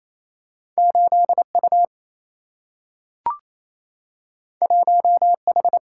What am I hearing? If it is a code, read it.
Morse code, heard as a keyed tone, about 28 wpm, 700 Hz. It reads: 8U 1H
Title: Single-Letter Single-Number 28wpm